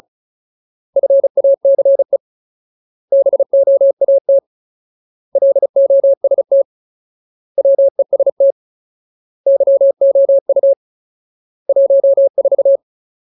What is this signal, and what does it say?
Morse code, heard as a keyed tone, about 34 wpm, 550 Hz. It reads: FACE BOAT LOST WEST YOU 14
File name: Ultimate Word List / 35WPM Ultimate